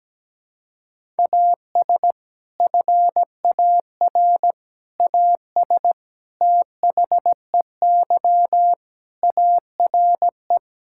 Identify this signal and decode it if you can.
Morse code, heard as a keyed tone, about 17 wpm, 700 Hz. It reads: AS FAR AS THEY ARE
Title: Sentences from Top 1000 Words - Review Part 6 - 17wpm